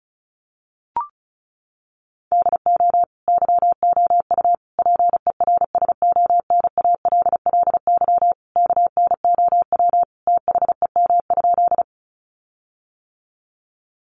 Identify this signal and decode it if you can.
Morse code, heard as a keyed tone, about 34 wpm, 700 Hz. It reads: DO YOU PERSONALLY KNOW THEM?
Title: Sentences from Top 1400 Words 35wpm